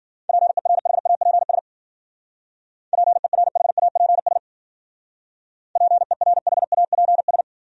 Morse code, heard as a keyed tone, about 50 wpm, 700 Hz. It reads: PERHAPS PERHAPS PERHAPS
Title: Speed-Racing Top 500 Words 40wpm